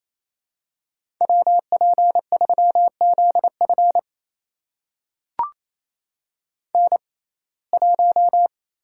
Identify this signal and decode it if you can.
Morse code, heard as a keyed tone, about 28 wpm, spaced as for 20 wpm, 700 Hz. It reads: WP3ZF N1
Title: USA Call Signs - 5 Characters - 6x Character Spacing - 28wpm